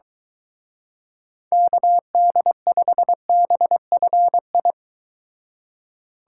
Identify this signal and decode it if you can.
Morse code, heard as a keyed tone, about 23 wpm, 700 Hz. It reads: KD5BFI